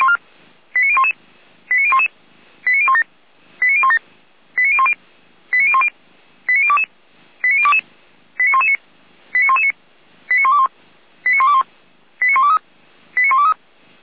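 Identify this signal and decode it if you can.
Five-tone selective call (ZVEI), heard as quick runs of five tones, tones of about 70 ms. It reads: 89011 89011 89017 89017 89010 89010 89022 89022 89119 89119 89121 89121 89123 89123
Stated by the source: ZVEI five-tone selective call 89001 received on SDR, fire brigade Feuerwehr Orlamünde.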